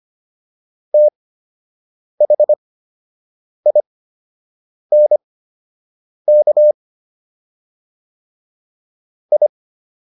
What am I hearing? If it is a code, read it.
Morse code, heard as a keyed tone, about 25 wpm, spaced as for 7 wpm, 600 Hz. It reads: THINK I